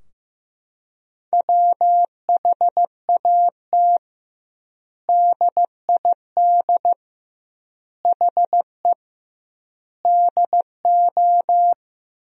Morse code, heard as a keyed tone, about 15 wpm, 700 Hz. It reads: WHAT DID HE DO
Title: 2x Word-Spacing - Sentences from Top 100 Words - 15wpm